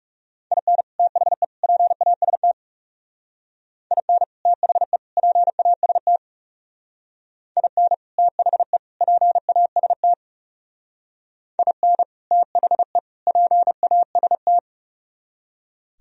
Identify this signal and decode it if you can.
Morse code, heard as a keyed tone, about 37 wpm, 700 Hz. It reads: IN THE PAST IN THE PAST IN THE PAST IN THE PAST
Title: Speed-Racing - Sets of 3 Words 30wpm